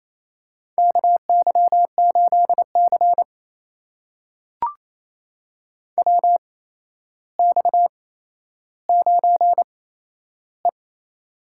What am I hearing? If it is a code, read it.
Morse code, heard as a keyed tone, about 28 wpm, spaced as for 12 wpm, 700 Hz. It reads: KY8C WX9E
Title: USA Call Signs - 4 Characters - 8x Character Spacing - 28wpm